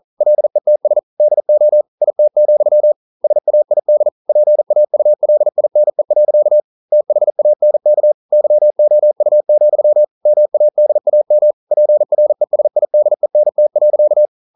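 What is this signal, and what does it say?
Morse code, heard as a keyed tone, about 42 wpm, 600 Hz. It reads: LETS DO IT, SAID PAULINE. THANK YOU, MADAM PRESIDENT.
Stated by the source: Morse code practice at 41 wpm, from CW med Gnister.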